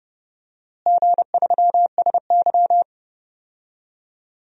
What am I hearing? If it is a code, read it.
Morse code, heard as a keyed tone, about 30 wpm, 700 Hz. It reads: G3SY
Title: DXCC-100 Call Signs - 4 Characters - Repeated 2x - 30wpm